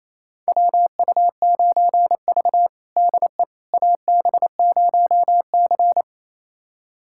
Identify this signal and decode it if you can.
Morse code, heard as a keyed tone, about 28 wpm, 700 Hz. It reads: WU9V DE AB0C
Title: USA Callsign DE Callsign - 4 Characters - 28wpm